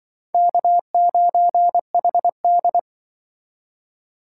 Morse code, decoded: K9HD

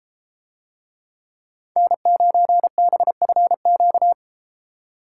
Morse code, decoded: N9BFQ

